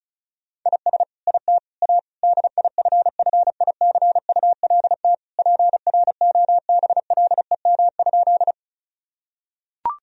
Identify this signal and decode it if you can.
Morse code, heard as a keyed tone, about 34 wpm, 700 Hz. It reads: IS IT A DIFFICULT PROBLEM?
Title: Sentences from Top 1000 Words - Review Part 2 - 35wpm